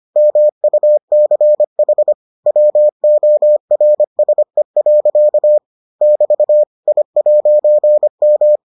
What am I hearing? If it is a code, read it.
Morse code, heard as a keyed tone, about 25 wpm, 600 Hz. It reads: MUCH WORSE. <BT> I'M